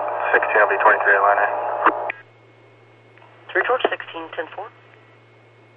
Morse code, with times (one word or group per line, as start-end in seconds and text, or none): none